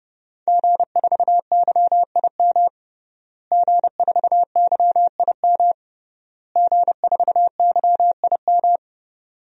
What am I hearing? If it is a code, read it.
Morse code, heard as a keyed tone, about 32 wpm, 700 Hz. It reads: G4YIM G4YIM G4YIM
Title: DXCC-100 Call Signs - 5 Characters - Repeated 3x - 30wpm